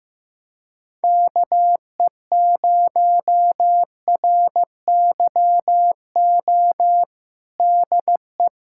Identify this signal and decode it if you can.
Morse code, heard as a keyed tone, about 15 wpm, 700 Hz. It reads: KE0RYO DE
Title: USA Callsign DE Callsign 15wpm